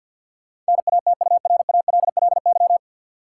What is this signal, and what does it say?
Morse code, heard as a keyed tone, about 52 wpm, 700 Hz. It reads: NATURALLY
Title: Sentences from Top 1300 Words - New Word Drill - 50wpm